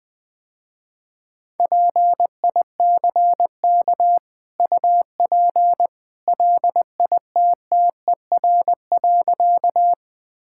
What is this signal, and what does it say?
Morse code, heard as a keyed tone, about 20 wpm, 700 Hz. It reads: PICK UP LITTER.
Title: Easy Sentences with Hints - 20wpm